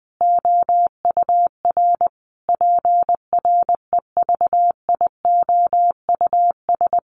Morse code, heard as a keyed tone, about 20 wpm, 700 Hz. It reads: OUR PREVIOUS
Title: Sentences from Top 1500 Words 20wpm